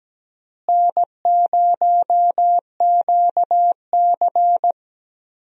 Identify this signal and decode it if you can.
Morse code, heard as a keyed tone, about 17 wpm, 700 Hz. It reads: N0QC
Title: USA Call Signs - 4 Characters - Repeated 3x - 17wpm